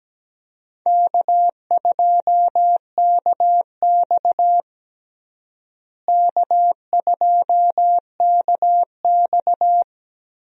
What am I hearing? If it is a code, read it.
Morse code, heard as a keyed tone, about 17 wpm, 700 Hz. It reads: K2KX K2KX
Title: USA Call Signs - 4 Characters - Repeated 3x - 17wpm